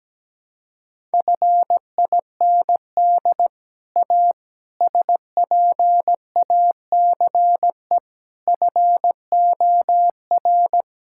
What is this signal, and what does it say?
Morse code, heard as a keyed tone, about 17 wpm, 700 Hz. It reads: FIND A SPACE FOR
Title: Sentences from Top 1000 Words - Review Part 5 - 17wpm